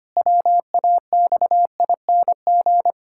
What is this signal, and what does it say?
Morse code, heard as a keyed tone, about 25 wpm, 700 Hz. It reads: WAXING
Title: Pangram Challenge v2 - 25wpm